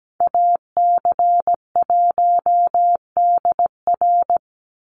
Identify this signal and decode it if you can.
Morse code, heard as a keyed tone, about 17 wpm, 700 Hz. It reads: AC1DR
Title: USA Call Signs 17wpm